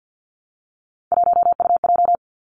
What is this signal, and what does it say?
Morse code, heard as a keyed tone, about 50 wpm, 700 Hz. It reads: 1UJ
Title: Three Challenging Letter-Number Combinations 50wpm